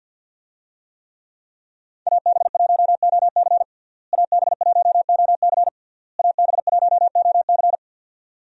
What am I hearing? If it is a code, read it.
Morse code, heard as a keyed tone, about 52 wpm, 700 Hz. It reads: AB1OC AB1OC AB1OC
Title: USA Call Signs - 5 Characters - Repeated 3x - 50wpm